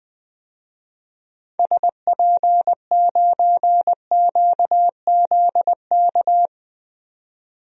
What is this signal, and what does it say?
Morse code, heard as a keyed tone, about 21 wpm, 700 Hz. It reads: SP9QZK
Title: DXCC-100 Call Signs - 6 Characters - 20wpm